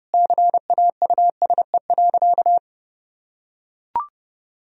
Morse code, heard as a keyed tone, about 32 wpm, 700 Hz. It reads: CAUSE.
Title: Sentences from Top 1000 Words - Review Part 4 - 30wpm